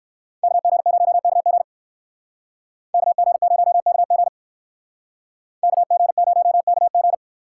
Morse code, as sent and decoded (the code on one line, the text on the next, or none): -.- --. ----- -.- --. / -.- --. ----- -.- --. / -.- --. ----- -.- --.
KG0KG KG0KG KG0KG